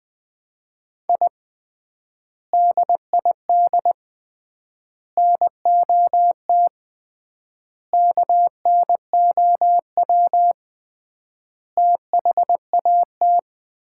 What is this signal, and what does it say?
Morse code, heard as a keyed tone, about 20 wpm, 700 Hz. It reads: I DID NOT KNOW THAT
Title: 3x Word-Spacing - Sets of 5 Words - 20wpm